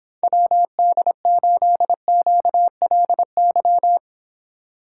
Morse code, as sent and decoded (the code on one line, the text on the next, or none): .-- -.. ---.. --.- .-.. -.--
WD8QLY